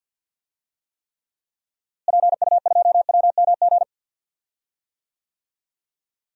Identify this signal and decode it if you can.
Morse code, heard as a keyed tone, about 53 wpm, 700 Hz. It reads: PU2WMG
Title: DXCC-100 Call Signs - 6 Characters - Repeated 2x - 50wpm